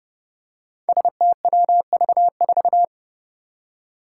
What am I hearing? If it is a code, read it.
Morse code, heard as a keyed tone, about 30 wpm, 700 Hz. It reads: STWV4
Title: Five Challenging Letter-Number Combinations 30wpm